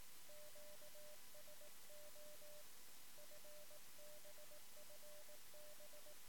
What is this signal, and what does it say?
Morse code, heard as a keyed tone, about 18 wpm, 600 Hz. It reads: QSO FB FB
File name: sm7lcb / 1 oh2pu-p 13cm